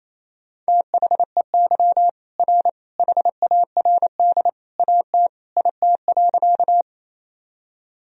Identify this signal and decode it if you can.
Morse code, heard as a keyed tone, about 28 wpm, 700 Hz. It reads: THEY R HARD AT IT.
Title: Sentences from Top 300 Words and Abbreviations 28wpm